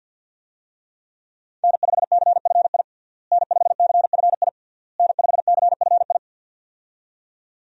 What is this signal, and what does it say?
Morse code, heard as a keyed tone, about 46 wpm, 700 Hz. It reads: N5CFI N5CFI N5CFI
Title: USA Call Signs - 5 Characters - Repeated 3x - 50wpm